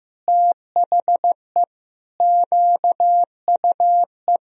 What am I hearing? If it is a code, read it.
Morse code, heard as a keyed tone, about 16 wpm, 700 Hz. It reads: THE QUE
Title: Sentences from Top 1000 Words - Review Part 2 - 15wpm